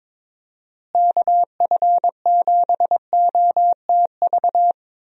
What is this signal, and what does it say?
Morse code, heard as a keyed tone, about 21 wpm, 700 Hz. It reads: KF7OTV
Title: USA Call Signs - 6 Characters - Repeated 2x - 22wpm